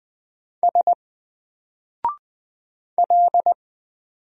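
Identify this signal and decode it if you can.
Morse code, heard as a keyed tone, about 20 wpm, 700 Hz. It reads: S L
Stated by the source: Single letter and number Morse drill at 20 wpm, from N4LCD.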